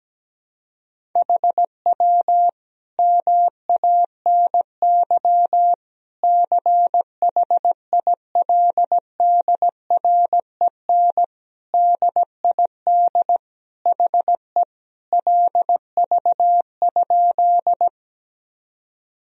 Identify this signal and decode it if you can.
Morse code, heard as a keyed tone, about 17 wpm, 700 Hz. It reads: HW MANY CHILDREN DID HE LV?
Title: Sentences from Top 400 Words and Abbreviations 17wpm